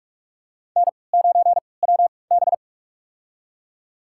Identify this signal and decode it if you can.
Morse code, heard as a keyed tone, about 46 wpm, spaced as for 25 wpm, 690 Hz. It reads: N9WB
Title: USA Call Signs - 4 Characters - 3x Character Spacing - 45wpm